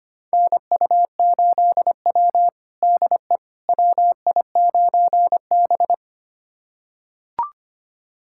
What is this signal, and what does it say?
Morse code, heard as a keyed tone, about 25 wpm, 700 Hz. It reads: NU8W DE WI9B
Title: USA Callsign DE Callsign - 4 Characters - 25wpm